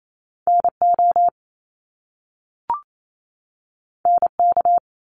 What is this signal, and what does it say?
Morse code, heard as a keyed tone, about 28 wpm, 700 Hz. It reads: NO NK
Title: Top Two Letter Combinations in Words 28wpm